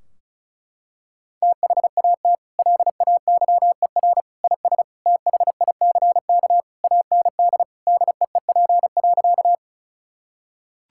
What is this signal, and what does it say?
Morse code, heard as a keyed tone, about 36 wpm, 700 Hz. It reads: THAT LAYER IS THICK AND DEEP.
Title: Sentences from Top 1600 Words 35wpm